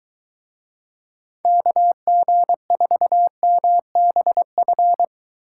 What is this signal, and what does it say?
Morse code, heard as a keyed tone, about 23 wpm, 700 Hz. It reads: KG4MBF